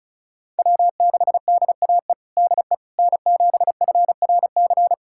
Morse code, heard as a keyed tone, about 36 wpm, 700 Hz. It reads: W6DAE DE N7FRC
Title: USA Callsign DE Callsign - 5 Characters - 35wpm